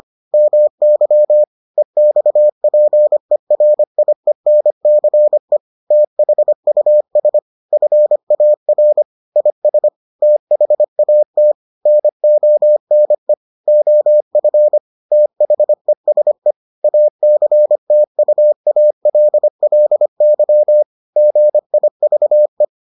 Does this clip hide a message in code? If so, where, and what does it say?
Yes: Morse code from 0.3 to 22.6 s, heard as a keyed tone, about 25 wpm, 600 Hz. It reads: MY EXPERIENCE THUS FAR IS THAT NONE OF THESE ACTUALLY GIVE